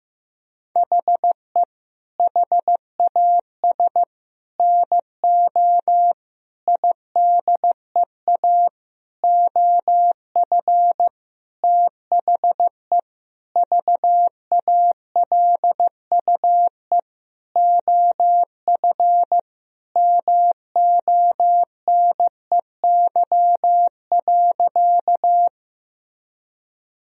Morse code, heard as a keyed tone, about 15 wpm, 700 Hz. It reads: HE HAS NO IDEA OF THE VALUE OF MONEY.
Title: Sentences from Top 1000 Words - Review Part 3 - 15wpm